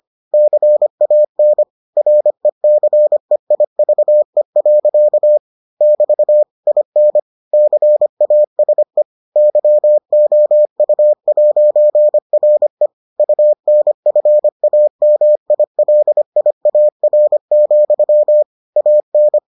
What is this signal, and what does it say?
Morse code, heard as a keyed tone, about 25 wpm, 600 Hz. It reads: CAN RECEIVE. <BT> IN CASE YOU'RE UNFAMILIAR, AN